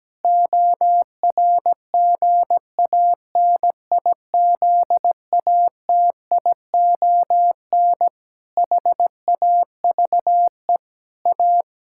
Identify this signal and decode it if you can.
Morse code, heard as a keyed tone, about 17 wpm, 700 Hz. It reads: ORGANIZATION HAVE A
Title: Sentences from Top 1000 Words - Review Part 6 - 17wpm